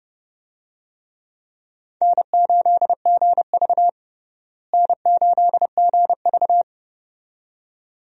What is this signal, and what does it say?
Morse code, heard as a keyed tone, about 32 wpm, 700 Hz. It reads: N8GV N8GV
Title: USA Call Signs - 4 Characters - Repeated 2x - 30wpm